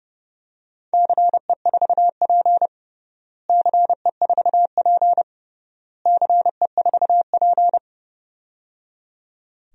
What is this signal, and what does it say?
Morse code, heard as a keyed tone, about 29 wpm, 700 Hz. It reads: CE4P CE4P CE4P
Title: DXCC-100 Call Signs - 4 Characters - Repeated 3x - 30wpm